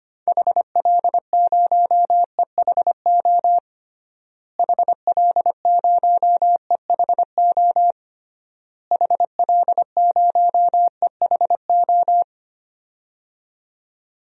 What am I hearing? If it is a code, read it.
Morse code, heard as a keyed tone, about 25 wpm, 700 Hz. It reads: HL0EHO HL0EHO HL0EHO
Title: DXCC-100 Call Signs - 6 Characters - Repeated 3x - 25wpm